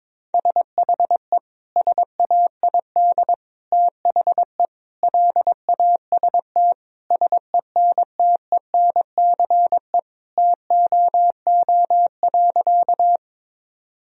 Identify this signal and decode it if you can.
Morse code, heard as a keyed tone, about 22 wpm, 700 Hz. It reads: SHE SAID THE LAST SENTENCE TOO.
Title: Sentences from Top 1000 Words - Review Part 1 - 22wpm